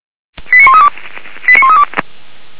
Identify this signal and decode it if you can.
Five-tone selective call (ZVEI), heard as quick runs of five tones, tones of about 70 ms. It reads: 89132 89132
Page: /464/SDR/ZVEI/